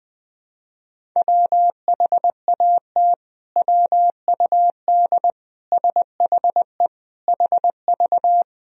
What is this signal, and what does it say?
Morse code, heard as a keyed tone, about 21 wpm, 700 Hz. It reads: WHAT WUD SHE HV
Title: Sentences from Top 100 Words and Abbreviations 20wpm